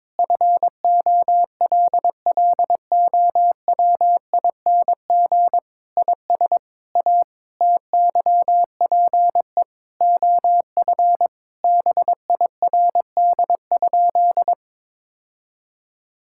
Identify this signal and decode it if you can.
Morse code, heard as a keyed tone, about 22 wpm, 700 Hz. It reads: FOLLOWING IS A TYPE OF BIRD?
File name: practice / 22wpm / Sentences with 8-Word Phrases - 22wpm